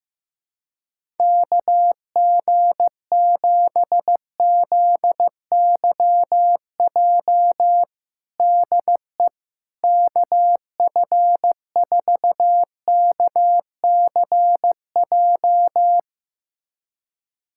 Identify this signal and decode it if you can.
Morse code, heard as a keyed tone, about 16 wpm, 700 Hz. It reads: KG7ZYJ DE KF4KCJ
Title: USA Callsign DE Callsign - 6 Characters - 15wpm